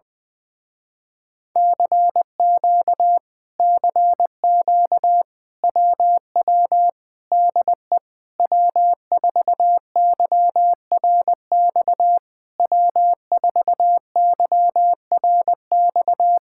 Morse code, heard as a keyed tone, about 20 wpm, 700 Hz. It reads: CQ CQ WW DE W4YRX W4YRX
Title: Speed-Racing Common QSO Phrases - 20wpm